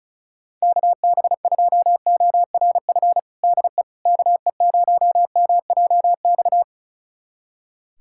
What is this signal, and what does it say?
Morse code, heard as a keyed tone, about 36 wpm, 700 Hz. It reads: KB2ORF DE KE0MJX
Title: USA Callsign DE Callsign - 6 Characters - 35wpm